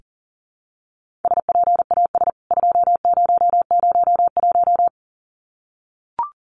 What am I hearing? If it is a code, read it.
Morse code, heard as a keyed tone, about 40 wpm, 700 Hz. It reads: SPAS 2001
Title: CWT Exchange 40wpm